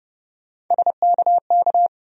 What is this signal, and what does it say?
Morse code, heard as a keyed tone, about 31 wpm, 700 Hz. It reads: SKK
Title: USA Call Sign Postfixes - 3 Letters 30wpm